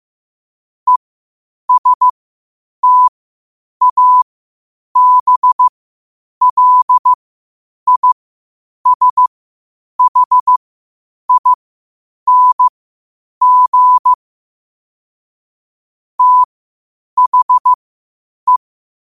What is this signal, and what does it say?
Morse code, heard as a keyed tone, about 15 wpm, spaced as for 8 wpm, 1000 Hz. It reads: ESTABLISHING THE
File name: qotdfiles / 8wpm